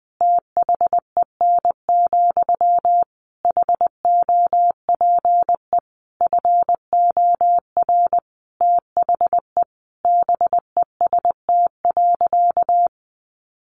Morse code, heard as a keyed tone, about 20 wpm, 700 Hz. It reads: THEN, HOPE FOR THE BEST.
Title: Sentences from Top 1000 Words 20wpm